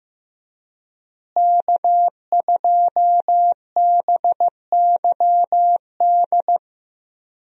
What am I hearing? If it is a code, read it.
Morse code, heard as a keyed tone, about 15 wpm, 700 Hz. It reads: K2BYD
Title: USA Call Signs - 5 Characters - Repeated 2x - 15wpm